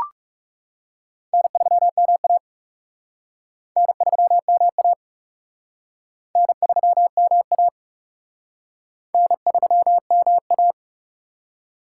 Morse code, heard as a keyed tone, about 40 wpm, 700 Hz. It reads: N3MA N3MA N3MA N3MA